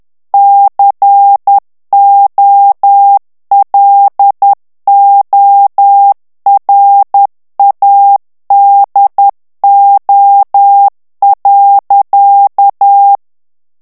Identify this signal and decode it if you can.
Morse code, heard as a keyed tone, about 10 wpm, 800 Hz. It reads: COLORADO.
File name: ANGLAIS texte clair 3 vitesse 10 mots minute